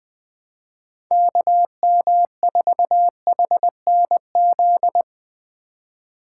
Morse code, heard as a keyed tone, about 23 wpm, 700 Hz. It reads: KM4HNZ
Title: Speed Racing - USA Call Signs 20wpm